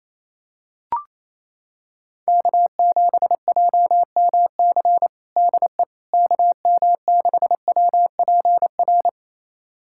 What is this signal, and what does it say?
Morse code, heard as a keyed tone, about 28 wpm, 700 Hz. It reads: K7JMC DE KM6WPR
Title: USA Callsign DE Callsign 28wpm